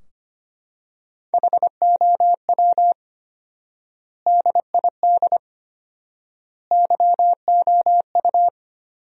Morse code, heard as a keyed tone, about 25 wpm, 700 Hz. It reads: HOW DID YOU
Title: 4x Word-Spacing - Sentences from Top 200 Words - 25wpm